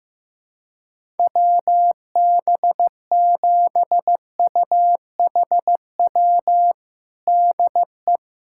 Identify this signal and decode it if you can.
Morse code, heard as a keyed tone, about 15 wpm, 700 Hz. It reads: WB7UHW DE
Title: USA Callsign DE Callsign 15wpm